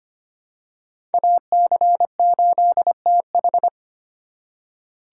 Morse code, heard as a keyed tone, about 25 wpm, 700 Hz. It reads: AC8TH
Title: USA Call Signs - 4x Character Spacing - 25wpm